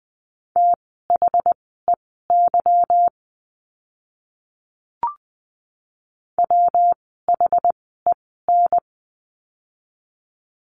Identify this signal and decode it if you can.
Morse code, heard as a keyed tone, about 20 wpm, spaced as for 14 wpm, 700 Hz. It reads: THEY WHEN
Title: 185 Lesson 30f - Letter b - Words 20wpm